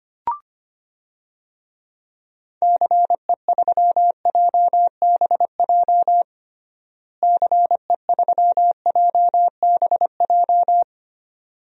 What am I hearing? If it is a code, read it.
Morse code, heard as a keyed tone, about 25 wpm, 700 Hz. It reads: CE3JBJ CE3JBJ
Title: DXCC-100 Call Signs - 6 Characters - Repeated 2x - 25wpm